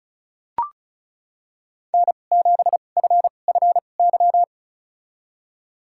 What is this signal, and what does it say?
Morse code, heard as a keyed tone, about 34 wpm, spaced as for 25 wpm, 700 Hz. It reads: N7FFY
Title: USA Call Signs - 2x Character Spacing - 35wpm